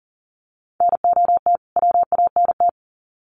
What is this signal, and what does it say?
Morse code, heard as a keyed tone, about 39 wpm, 700 Hz. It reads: NOT WANT
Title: Sets of 2 Words 40wpm